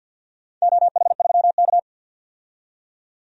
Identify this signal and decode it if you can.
Morse code, heard as a keyed tone, about 51 wpm, 700 Hz. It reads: OH3K
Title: DXCC-100 Call Signs - 4 Characters - 50wpm